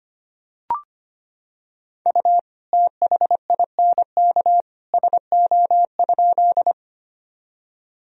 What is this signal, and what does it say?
Morse code, heard as a keyed tone, about 25 wpm, 700 Hz. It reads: U THINK SO?
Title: Sentences from Top 200 Words and Abbreviations 25wpm